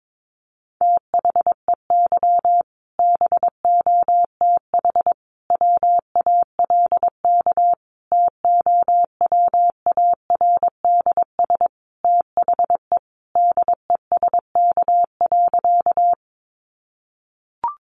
Morse code, heard as a keyed tone, about 22 wpm, 700 Hz. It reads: THEY BOTH WALK TOWARDS THE DESK.
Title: Additional Easy Phrases 22wpm